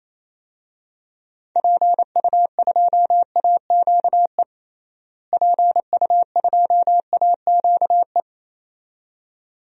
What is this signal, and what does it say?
Morse code, heard as a keyed tone, about 27 wpm, 700 Hz. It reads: PU2AQE PU2AQE
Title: DXCC-100 Call Signs - 6 Characters - Repeated 2x - 28wpm